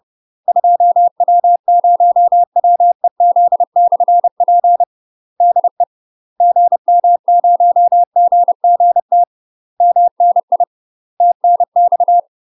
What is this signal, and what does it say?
Morse code, heard as a keyed tone, about 30 wpm, 700 Hz. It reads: 2W0WEZ/P DE GM0GGT MNI TNX
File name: UK MASTER QSOs TRAINING / 30WPM / UK MASTER QSOs 0049